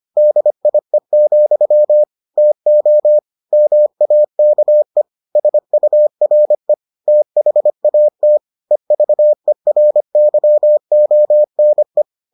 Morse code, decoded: DIE, TO MAKE SURE THAT EVERYONE

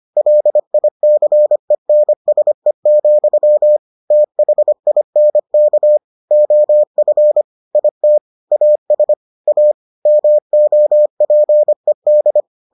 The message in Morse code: LICENSE, THINK OF IT AS A MOPED

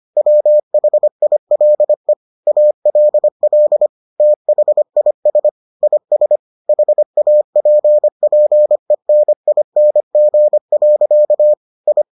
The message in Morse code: WHILE ALL THIS IS HAPPENING. I

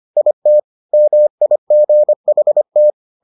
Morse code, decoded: IT MIGHT

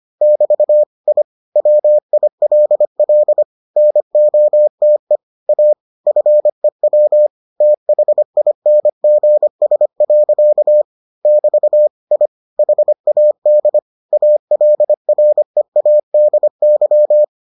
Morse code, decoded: <BT> I WILL NOTE A FEW THINGS. <BT> I HAD ALREADY

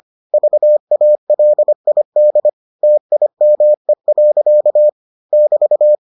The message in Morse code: VALID TIME. <BT>

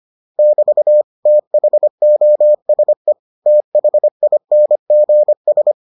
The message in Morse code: <BT> THOSE THINGS

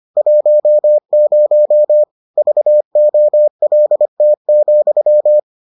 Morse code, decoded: 10 VOLT,

